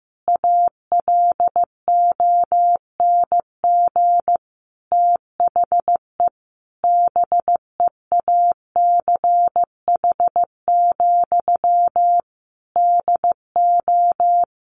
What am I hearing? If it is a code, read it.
Morse code, heard as a keyed tone, about 15 wpm, 700 Hz. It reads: ALONG THE BEACH, DO
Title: Sentences from Top 1500 Words 15wpm